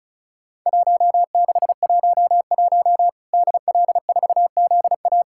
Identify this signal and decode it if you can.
Morse code, heard as a keyed tone, about 36 wpm, 700 Hz. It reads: 1611 DL4ZA
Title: Worked All Europe DX Contest - QTC Exchange - 35wpm